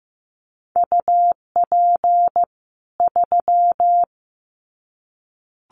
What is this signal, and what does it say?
Morse code, heard as a keyed tone, about 15 wpm, 700 Hz. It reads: UP 3
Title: Common QSO Phrases 15wpm